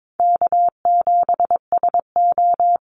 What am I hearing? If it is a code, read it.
Morse code, heard as a keyed tone, about 22 wpm, 700 Hz. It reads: K7SO
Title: NA SOTA Activator Callsigns 22wpm